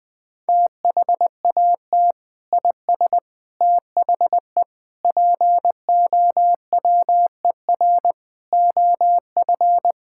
Morse code, heard as a keyed tone, about 19 wpm, 700 Hz. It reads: THAT IS THE POWER OF
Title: Sentences from Top 1000 Words - Review Part 5 - 20wpm